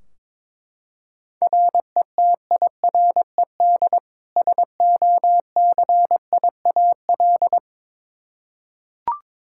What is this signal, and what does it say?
Morse code, heard as a keyed tone, about 22 wpm, 700 Hz. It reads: RETIRED SOCIAL
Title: K1USN SST QSO Practice 22wpm